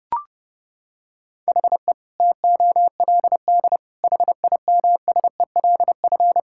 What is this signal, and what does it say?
Morse code, heard as a keyed tone, about 30 wpm, 700 Hz. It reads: HE TOLD HIMSELF
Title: Sentences from Top 1600 Words 30wpm